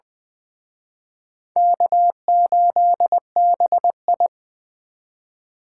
Morse code, heard as a keyed tone, about 23 wpm, 700 Hz. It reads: K8BI
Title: Speed Racing - USA 1x2 Call Signs 20wpm